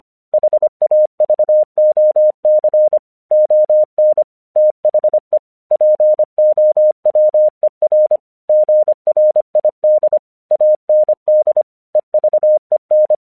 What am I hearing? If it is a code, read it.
Morse code, heard as a keyed tone, about 25 wpm, 610 Hz. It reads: HAVOC ON THE POWER GRID AND EVEN
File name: 1 English 25wpm / AQC06